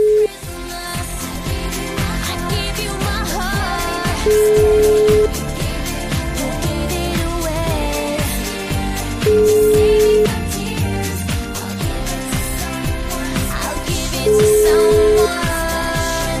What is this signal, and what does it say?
Ringback tone (Europe, 425 Hz), with noise.